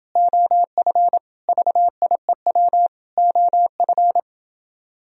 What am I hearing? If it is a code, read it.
Morse code, heard as a keyed tone, about 30 wpm, 700 Hz. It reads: OF VIEW OF